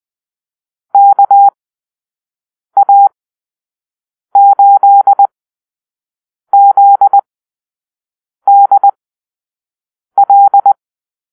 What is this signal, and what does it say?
Morse code, heard as a keyed tone, about 20 wpm, spaced as for 6 wpm, 800 Hz. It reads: KA8ZDL
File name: arrl / Call-Sign Drill-5